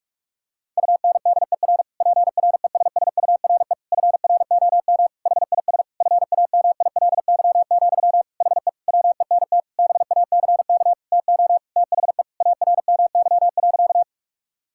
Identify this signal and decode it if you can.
Morse code, heard as a keyed tone, about 42 wpm, 700 Hz. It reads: UNDER PRESSURE FROM HIS FAMILY, HE WENT BACK TO THE ARMY.
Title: Sentences from Top 1000 Words - Review Part 6 - 45wpm